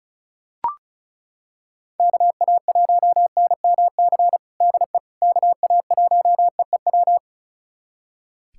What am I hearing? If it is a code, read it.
Morse code, heard as a keyed tone, about 36 wpm, 700 Hz. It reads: KA1NMC DE KA1EEW
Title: USA Callsign DE Callsign 35wpm